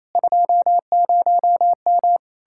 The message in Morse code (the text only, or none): 20M